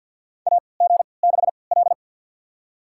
Morse code, heard as a keyed tone, about 52 wpm, spaced as for 29 wpm, 700 Hz. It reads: AG6L
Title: USA Call Signs - 3x Character Spacing - 50wpm